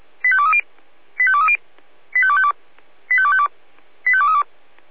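Five-tone selective call (ZVEI), heard as quick runs of five tones, tones of about 70 ms. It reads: 85230 85230 85252 85252 85232
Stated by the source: Received by SDR.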